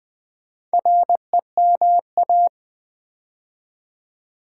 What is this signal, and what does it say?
Morse code, heard as a keyed tone, about 20 wpm, 700 Hz. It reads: REMA
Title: Build A Word - 2 Letters At A Time - 6-Letter Words - 20wpm